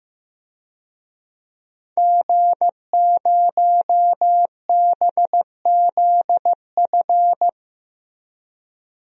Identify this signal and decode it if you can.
Morse code, heard as a keyed tone, about 15 wpm, 700 Hz. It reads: G0BZF
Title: DXCC-100 Call Signs - 5 Characters - Repeated 2x - 15wpm